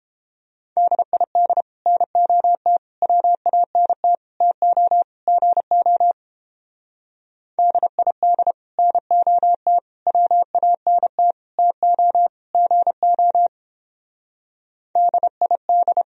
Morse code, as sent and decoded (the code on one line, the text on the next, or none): -.. .. -.. / -. --- - / .-- .- -. - / - --- / --. --- / -.. .. -.. / -. --- - / .-- .- -. - / - --- / --. --- / -.. .. -..
DID NOT WANT TO GO DID NOT WANT TO GO DID